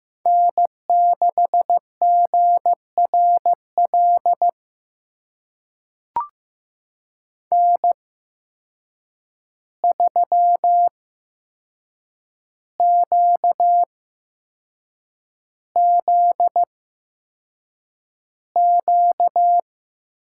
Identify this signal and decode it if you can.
Morse code, heard as a keyed tone, about 15 wpm, spaced as for 6 wpm, 700 Hz. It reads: N6GRL N3QZQ
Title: USA Call Signs - 5 Characters - 8x Character Spacing - 15wpm